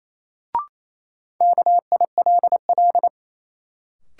Morse code, heard as a keyed tone, about 28 wpm, 700 Hz. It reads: KILL
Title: Sentences from Top 1400 Words - New Word Drill Alt - 28wpm